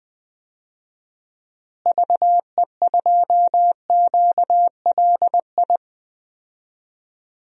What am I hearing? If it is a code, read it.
Morse code, heard as a keyed tone, about 21 wpm, 700 Hz. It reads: VE2QLI
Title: DXCC-100 Call Signs - 6 Characters - Repeated 3x - 20wpm